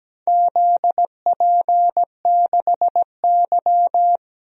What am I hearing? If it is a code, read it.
Morse code, heard as a keyed tone, about 17 wpm, 700 Hz. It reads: ZP6Y
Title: DXCC-100 Call Signs - 4 Characters - 17wpm